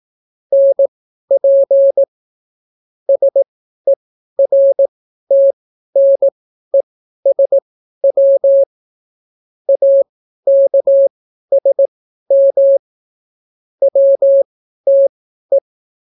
Morse code, heard as a keyed tone, about 18 wpm, spaced as for 12 wpm, 550 Hz. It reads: NP SERTNESW AKSM WTE